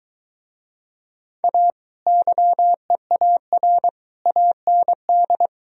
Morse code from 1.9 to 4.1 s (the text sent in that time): YEAR